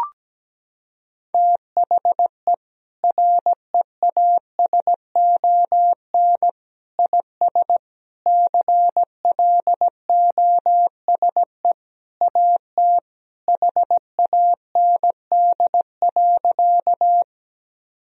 Morse code, heard as a keyed tone, about 17 wpm, 700 Hz. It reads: THE REASON IS CLOSE AT HAND.
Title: Sentences from Top 1000 Words - Review Part 1 - 17wpm